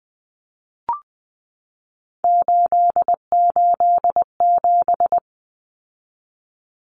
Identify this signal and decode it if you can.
Morse code, heard as a keyed tone, about 20 wpm, 700 Hz. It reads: 887
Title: Common Cut Numbers - 3-Digit 20wpm